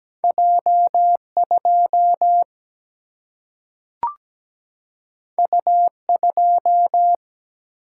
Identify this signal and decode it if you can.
Morse code, heard as a keyed tone, about 17 wpm, 700 Hz. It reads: J 2 U 2
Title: Single-Letter Single-Number 17wpm